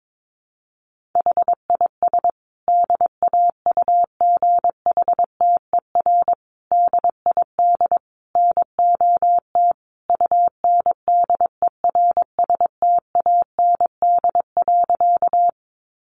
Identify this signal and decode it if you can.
Morse code, heard as a keyed tone, about 22 wpm, 700 Hz. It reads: HIS DAUGHTER DID NOT UNDERSTAND.
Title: Sentences from Top 1500 Words 22wpm